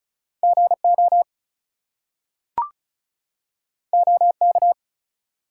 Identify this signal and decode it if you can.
Morse code, heard as a keyed tone, about 36 wpm, 700 Hz. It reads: GO OK
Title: Two Letter Words - 35wpm